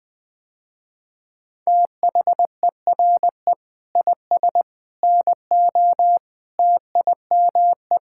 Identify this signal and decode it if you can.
Morse code, heard as a keyed tone, about 20 wpm, 700 Hz. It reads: THERE IS NO TIME
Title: Sentences with 3-Word Phrases - Spoken Beginning - 20wpm